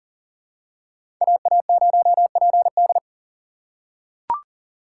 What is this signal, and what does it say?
Morse code, heard as a keyed tone, about 40 wpm, 700 Hz. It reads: AA0PD
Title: USA Call Signs - 5 Characters - 3x Character Spacing - 40wpm